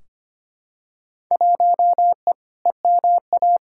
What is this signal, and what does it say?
Morse code, heard as a keyed tone, about 25 wpm, 700 Hz. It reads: 1E EMA
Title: Field Day Exchanges Encore - 25wpm